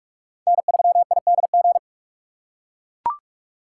Morse code, heard as a keyed tone, about 41 wpm, 700 Hz. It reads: N3IDG